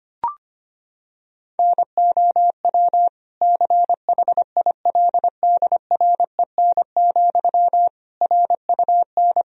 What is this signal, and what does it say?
Morse code, heard as a keyed tone, about 25 wpm, 700 Hz. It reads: NOW CHILDREN, RUN
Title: Sentences from Top 1000 Words - Review Part 1 - 25wpm